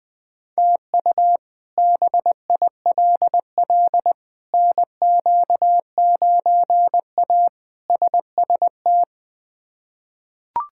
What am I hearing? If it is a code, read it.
Morse code, heard as a keyed tone, about 20 wpm, 700 Hz. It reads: TU BILL NQ9A SST
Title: K1USN SST QSO Practice - 20wpm